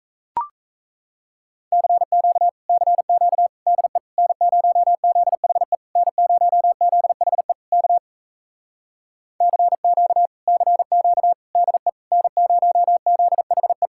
Morse code, heard as a keyed tone, about 37 wpm, 700 Hz. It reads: CQ CQ DE N0ZHE N0ZHE K CQ CQ DE N0ZHE